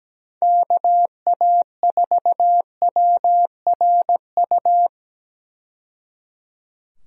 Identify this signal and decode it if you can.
Morse code, heard as a keyed tone, about 17 wpm, 700 Hz. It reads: KA4WRU
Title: USA Call Signs - 6 Characters - 17wpm